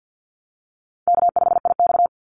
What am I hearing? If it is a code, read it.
Morse code, heard as a keyed tone, about 49 wpm, 700 Hz. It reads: K5IX
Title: Easy Call Signs Repeated 2x - 50wpm